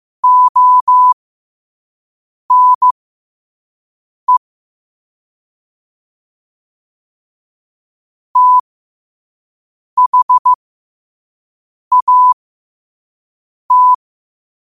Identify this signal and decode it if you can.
Morse code, heard as a keyed tone, about 15 wpm, spaced as for 5 wpm, 1000 Hz. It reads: ONE THAT